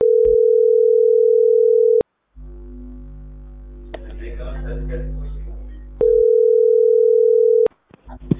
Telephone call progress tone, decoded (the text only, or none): ringback tone